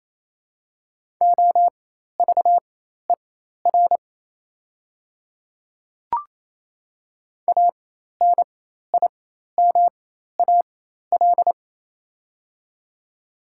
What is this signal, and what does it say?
Morse code, heard as a keyed tone, about 28 wpm, spaced as for 13 wpm, 700 Hz. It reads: OVER ANIMAL
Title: 4x Character Spacing - Top 200 Words - 28wpm